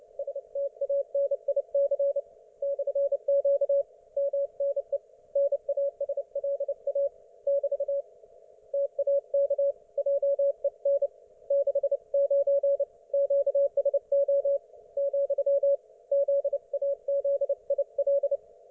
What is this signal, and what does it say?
Morse code, heard as a keyed tone, about 29 wpm, 560 Hz. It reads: STANIC /Q MNE NASLA <BT> TAK JEN 69 QSO , ZAZIL